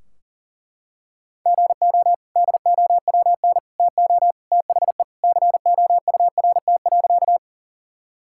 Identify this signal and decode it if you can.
Morse code, heard as a keyed tone, about 37 wpm, 700 Hz. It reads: GO DOWN TO THE COURT.
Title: Sentences from Top 1600 Words 40wpm